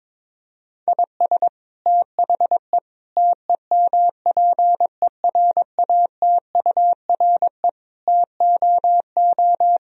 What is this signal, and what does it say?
Morse code, heard as a keyed tone, about 22 wpm, 700 Hz. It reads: IS THE TEMPERATURE TOO
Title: Sentences from Top 1000 Words - Review Part 4 - 22wpm